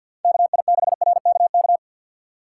Morse code, heard as a keyed tone, about 65 wpm, spaced as for 57 wpm, 700 Hz. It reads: KI6RKK
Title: Speed Racing - USA 2x3 Call Signs 50wpm